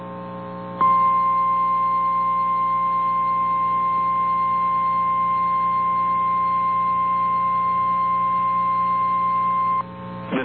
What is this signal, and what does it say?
Emergency Alert System (SAME) tones (the 1050 Hz attention tone). It is heard from 1.1 to 9.5 s.
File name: eas / KNCYKBIE / KNCYKBIE